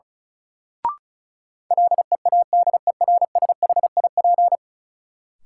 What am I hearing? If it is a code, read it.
Morse code, heard as a keyed tone, about 34 wpm, 700 Hz. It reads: LEADERSHIP
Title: Sentences from Top 1600 Words - New Word Drill Alt - 35wpm